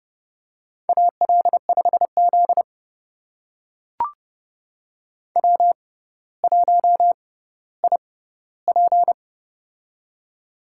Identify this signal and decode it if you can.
Morse code, heard as a keyed tone, about 32 wpm, spaced as for 16 wpm, 700 Hz. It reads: AL5Z W1IP
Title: USA Call Signs - 4 Characters - 6x Character Spacing - 30wpm